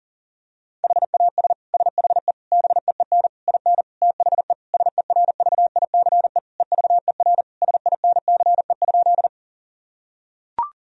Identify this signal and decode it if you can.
Morse code, heard as a keyed tone, about 40 wpm, 690 Hz. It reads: HAS SHE BEEN IN THE SERVICE EVER SINCE?
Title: Sentences from Top 1000 Words - Review Part 4 - 40wpm